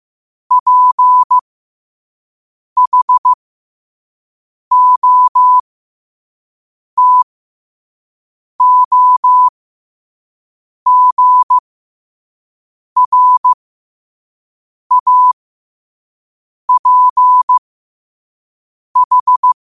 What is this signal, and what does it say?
Morse code, heard as a keyed tone, about 15 wpm, spaced as for 5 wpm, 1000 Hz. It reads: PHOTOGRAPH
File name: 5wpm